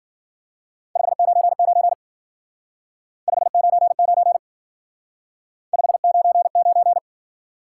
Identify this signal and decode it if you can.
Morse code, heard as a keyed tone, about 50 wpm, 700 Hz. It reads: 599 599 599
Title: Speed-Racing Top 90 QSO Elements 40wpm